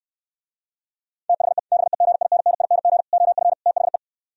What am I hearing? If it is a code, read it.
Morse code, heard as a keyed tone, about 68 wpm, 700 Hz. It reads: THE BEGINNING OF THE